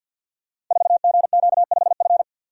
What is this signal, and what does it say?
Morse code, heard as a keyed tone, about 52 wpm, 690 Hz. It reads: 4GQ5F
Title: Code Groups - 5 Letter:Numbers - 50wpm